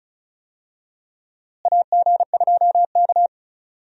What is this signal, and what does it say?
Morse code, heard as a keyed tone, about 36 wpm, 700 Hz. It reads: AG2K
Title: USA Call Signs - 4 Characters - Repeated 3x - 35wpm